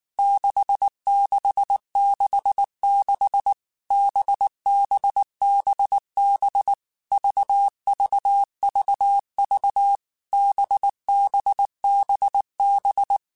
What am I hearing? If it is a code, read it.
Morse code, heard as a keyed tone, about 19 wpm, 800 Hz. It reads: 6666 BBBB VVVV BBBB
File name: Basic Letter Groups By7Groups / VB46 / VB46@18